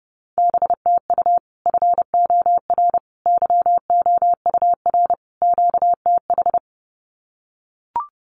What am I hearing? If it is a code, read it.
Morse code, heard as a keyed tone, about 30 wpm, 700 Hz. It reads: BTU FOR YOUR QTH
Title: Common QSO Phrases 30wpm